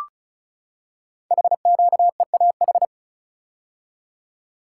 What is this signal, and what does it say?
Morse code, heard as a keyed tone, about 36 wpm, 700 Hz. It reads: HQEAH
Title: Code Groups - 5 Letters - 35wpm